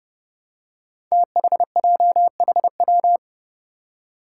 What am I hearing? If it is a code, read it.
Morse code, heard as a keyed tone, about 30 wpm, 700 Hz. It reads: THJHW
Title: Five Challenging Letter-Number Combinations 30wpm